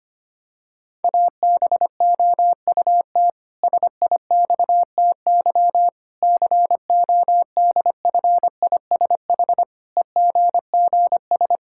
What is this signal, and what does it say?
Morse code, heard as a keyed tone, about 25 wpm, 700 Hz. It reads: ABOUT SIXTY CODFISH EGGS
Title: Pangram Challenge v2 - 25wpm